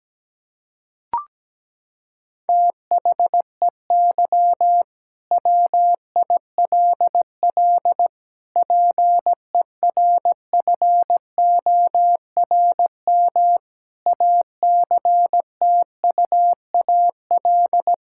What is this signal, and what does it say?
Morse code, heard as a keyed tone, about 17 wpm, 700 Hz. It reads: THEY WILL PERFORM ACTUAL
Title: Sentences from Top 1000 Words - Review Part 5 - 17wpm